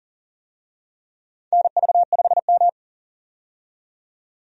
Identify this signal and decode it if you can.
Morse code, heard as a keyed tone, about 40 wpm, 700 Hz. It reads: NV5M
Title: USA Call Signs - 4 Characters - 40wpm